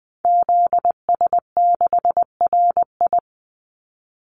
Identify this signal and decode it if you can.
Morse code, heard as a keyed tone, about 20 wpm, 700 Hz. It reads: ZS6RI
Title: Call Signs Repeated 3x - 20wpm